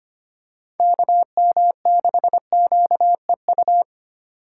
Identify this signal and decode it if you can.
Morse code, heard as a keyed tone, about 25 wpm, 700 Hz. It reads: KM6QEU
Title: USA Call Signs - 6 Characters - 3x Character Spacing - 25wpm